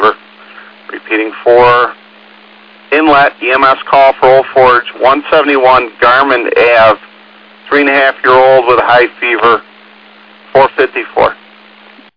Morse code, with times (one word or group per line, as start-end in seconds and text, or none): none